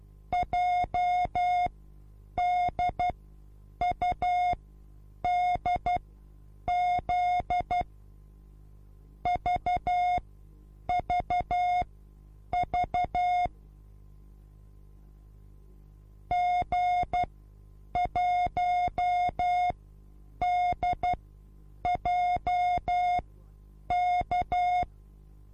Morse code, heard as a keyed tone, about 12 wpm, spaced as for 8 wpm, 720 Hz. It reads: JDUDZ VVV G1DJK
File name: repos / Morsekurs / DJ4UF Morsen24